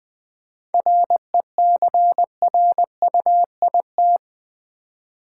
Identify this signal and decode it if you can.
Morse code, heard as a keyed tone, about 20 wpm, 700 Hz. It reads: RECRUIT
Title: 7 Letter Words - First 3 Letters Given - 20wpm